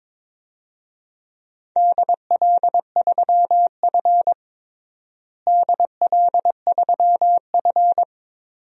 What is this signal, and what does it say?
Morse code, heard as a keyed tone, about 22 wpm, 700 Hz. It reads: DL3F DL3F
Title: DXCC-100 Call Signs - 4 Characters - Repeated 2x - 22wpm